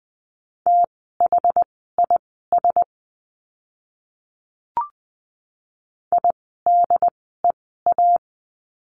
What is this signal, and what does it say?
Morse code, heard as a keyed tone, about 20 wpm, spaced as for 15 wpm, 700 Hz. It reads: THIS IDEA
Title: 108 Lesson 18f - Letter m - Words 20wpm